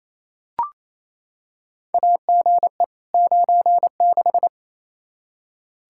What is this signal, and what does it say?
Morse code, heard as a keyed tone, about 28 wpm, 700 Hz. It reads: AGE 96
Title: Practice QSO - Exchange 3 - More Info - 28wpm